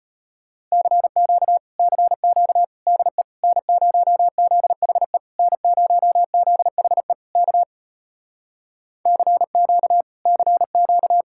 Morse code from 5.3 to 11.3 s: N0ZHE K CQ CQ

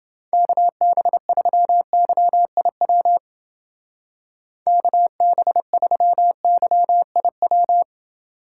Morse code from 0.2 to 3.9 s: KB3YIW

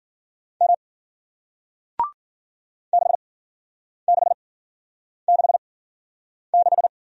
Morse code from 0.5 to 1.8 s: M

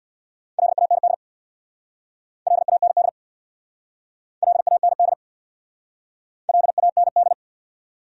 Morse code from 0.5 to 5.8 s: LAND LAND LAND

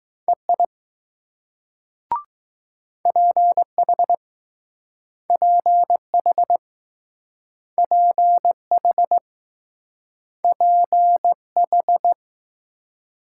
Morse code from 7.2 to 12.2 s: PH P H

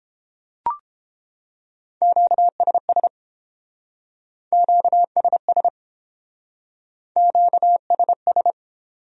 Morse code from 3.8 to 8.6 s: QSS QSS

